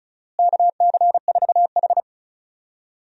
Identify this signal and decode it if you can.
Morse code, heard as a keyed tone, about 36 wpm, 700 Hz. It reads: KC4H
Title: USA Call Signs - 4 Characters - Repeated 2x - 35wpm